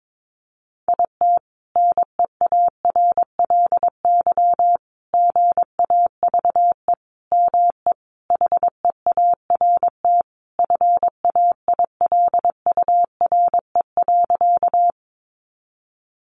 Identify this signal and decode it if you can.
Morse code, heard as a keyed tone, about 22 wpm, 700 Hz. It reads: IT NEARLY GAVE ME HEART FAILURE.
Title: Sentences from Top 1300 Words 22wpm